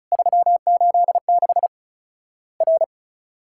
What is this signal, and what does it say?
Morse code, heard as a keyed tone, about 36 wpm, 700 Hz. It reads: 386 R
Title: Worked All Europe DX Contest - QTC Exchange - 35wpm